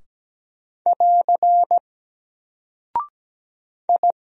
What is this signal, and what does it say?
Morse code, heard as a keyed tone, about 17 wpm, 700 Hz. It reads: <AR> I
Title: Single Character (Letter:Number:Punctuation:Prosign) Alt 17wpm